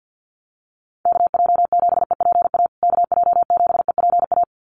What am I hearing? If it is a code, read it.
Morse code, heard as a keyed tone, about 50 wpm, 700 Hz. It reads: KJ7EPA KJ7EPA
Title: USA Call Signs Repeated 2x 50wpm